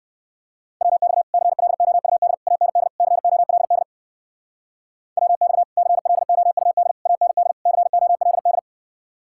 Morse code, heard as a keyed tone, about 59 wpm, 700 Hz. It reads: WX CLOUD AND COLD WX CLOUD AND COLD